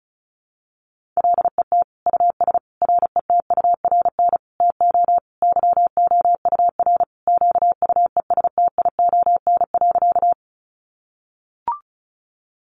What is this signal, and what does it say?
Morse code, heard as a keyed tone, about 34 wpm, 700 Hz. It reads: LET US RETURN TO YOUR QUESTION.
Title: Sentences from Top 800 Words 35wpm